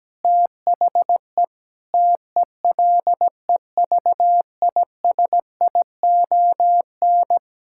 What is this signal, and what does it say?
Morse code, heard as a keyed tone, about 17 wpm, 700 Hz. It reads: THE TELEVISION
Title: Sentences from Top 1000 Words - Review Part 5 - 17wpm